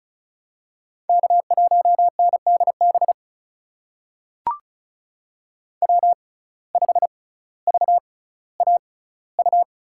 Morse code, heard as a keyed tone, about 36 wpm, spaced as for 18 wpm, 700 Hz. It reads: K1NDB W5VAU
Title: USA Call Signs - 6x Character Spacing - 35wpm